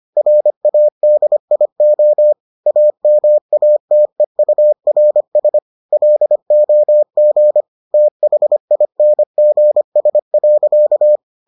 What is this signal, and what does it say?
Morse code, heard as a keyed tone, about 25 wpm, 600 Hz. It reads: RADIO AMATEURS LOG THINGS.